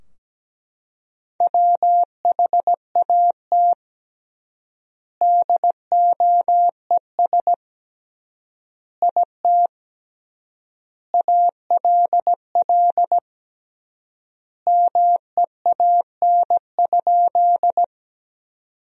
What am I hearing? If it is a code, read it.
Morse code, heard as a keyed tone, about 17 wpm, 700 Hz. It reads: WHAT DOES IT ALL MEAN?
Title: 3x Word-Spacing - Sentences from Top 200 Words - 17wpm